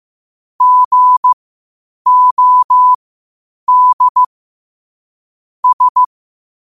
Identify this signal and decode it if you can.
Morse code, heard as a keyed tone, about 15 wpm, spaced as for 8 wpm, 1000 Hz. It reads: GOD S
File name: qotdfiles / 8wpm